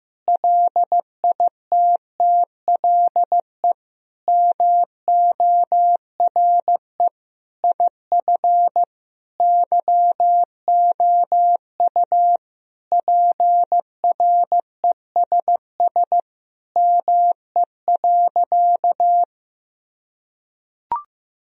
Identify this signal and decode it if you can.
Morse code, heard as a keyed tone, about 15 wpm, 700 Hz. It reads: LITTLE MORE IF YOU PRESS ME.
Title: Sentences from Top 1000 Words - Review Part 1 - 15wpm